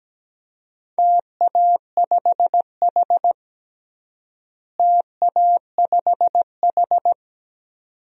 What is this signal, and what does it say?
Morse code, heard as a keyed tone, about 17 wpm, 700 Hz. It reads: TA5H TA5H
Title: DXCC-100 Call Signs - 4 Characters - Repeated 2x - 17wpm